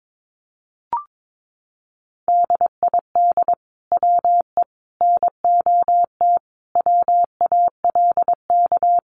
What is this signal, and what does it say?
Morse code, heard as a keyed tone, about 22 wpm, 700 Hz. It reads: DID WE NOT WALK
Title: Sentences from Top 1400 Words 22wpm